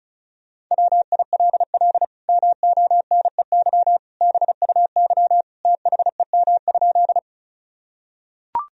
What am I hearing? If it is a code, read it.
Morse code, heard as a keyed tone, about 36 wpm, 700 Hz. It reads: WILL MONEY BUY THEM?
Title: Sentences from Top 1000 Words - Review Part 4 - 35wpm